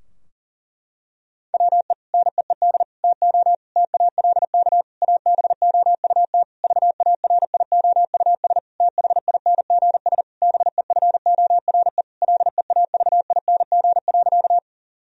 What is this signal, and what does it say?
Morse code, heard as a keyed tone, about 40 wpm, 700 Hz. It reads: WE NEED TO TALK ABOUT VARIOUS THINGS BEFORE LEAVING.
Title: Sentences from Top 1000 Words - Review Part 4 - 40wpm